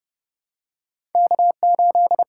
Morse code, heard as a keyed tone, about 31 wpm, 700 Hz. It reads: K8
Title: USA Call Signs - 5 Characters - 8x Character Spacing - 30wpm